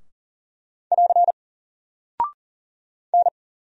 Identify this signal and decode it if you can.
Morse code, heard as a keyed tone, about 39 wpm, 700 Hz. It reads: <AR> N
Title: Single Character (Letter:Number:Punctuation:Prosign) Alt 40wpm